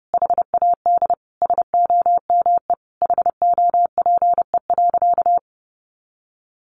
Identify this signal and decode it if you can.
Morse code, heard as a keyed tone, about 28 wpm, 700 Hz. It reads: HAD SOME HOPE.
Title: Sentences from Top 1000 Words 30wpm